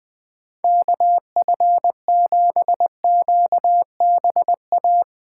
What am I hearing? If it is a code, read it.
Morse code, heard as a keyed tone, about 20 wpm, 700 Hz. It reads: KF7QBA
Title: USA Call Signs - 6 Characters - 3x Character Spacing - 20wpm